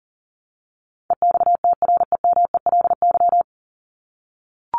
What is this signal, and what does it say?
Morse code, heard as a keyed tone, about 40 wpm, 700 Hz. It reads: EXTREMELY
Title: LY Suffix - Top 100 Words with Base Words 40wpm